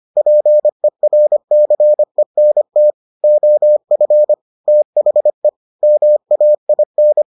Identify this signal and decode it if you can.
Morse code, heard as a keyed tone, about 25 wpm, 600 Hz. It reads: PERCENT OF THE MAIN